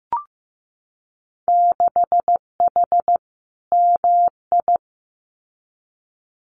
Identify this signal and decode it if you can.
Morse code, heard as a keyed tone, about 15 wpm, 700 Hz. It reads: 6H MI
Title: Winter Field Day Exchanges Encore 15wpm